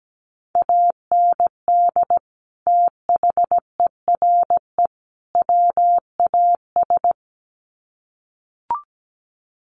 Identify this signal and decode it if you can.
Morse code, heard as a keyed tone, about 17 wpm, 700 Hz. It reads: AND THERE WAS
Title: Sets of 3 Words 17wpm